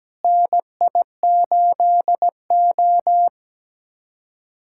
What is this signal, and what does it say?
Morse code, heard as a keyed tone, about 17 wpm, 700 Hz. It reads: NI8O
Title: USA Callsign DE Callsign - 4 Characters - 17wpm